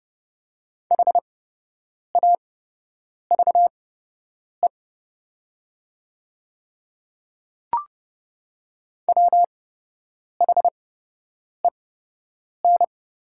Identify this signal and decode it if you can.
Morse code, heard as a keyed tone, about 30 wpm, spaced as for 8 wpm, 700 Hz. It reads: HAVE WHEN
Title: 8x Character Spacing - Top 100 Words - 30wpm